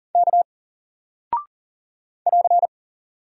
Morse code, heard as a keyed tone, about 40 wpm, 700 Hz. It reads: K <AR>
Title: Single Character (Letter:Number:Punctuation:Prosign) Alt 40wpm